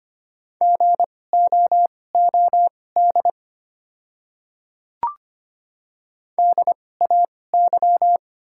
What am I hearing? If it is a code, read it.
Morse code, heard as a keyed tone, about 25 wpm, spaced as for 18 wpm, 690 Hz. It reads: GOOD DAY
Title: 2x Character Spacing - Top 300 Words - 25wpm